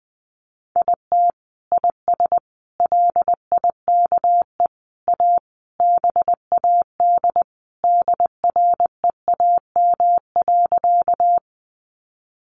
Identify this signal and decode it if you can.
Morse code, heard as a keyed tone, about 20 wpm, 700 Hz. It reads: IT IS LIKE A BAD DREAM.
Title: Sentences from Top 1500 Words 20wpm